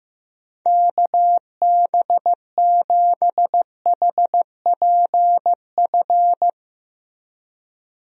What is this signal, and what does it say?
Morse code, heard as a keyed tone, about 15 wpm, 700 Hz. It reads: KB7HPF
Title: USA Call Signs - 6 Characters - Repeated 2x - 15wpm